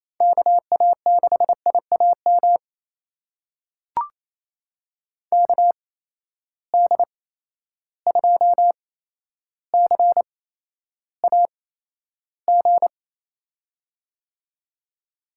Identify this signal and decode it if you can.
Morse code, heard as a keyed tone, about 28 wpm, spaced as for 12 wpm, 700 Hz. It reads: KA6IAM KD2CAG
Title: USA Call Signs - 6 Characters - 8x Character Spacing - 28wpm